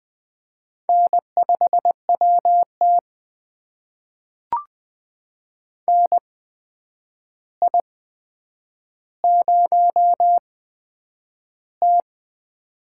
Morse code, heard as a keyed tone, about 20 wpm, spaced as for 9 wpm, 700 Hz. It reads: N5WT NI0T
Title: USA Call Signs - 4 Characters - 8x Character Spacing - 20wpm